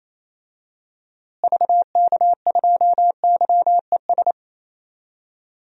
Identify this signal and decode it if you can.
Morse code, heard as a keyed tone, about 28 wpm, 700 Hz. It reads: VK2YES
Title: DXCC-100 Call Signs - 6 Characters - Repeated 2x - 28wpm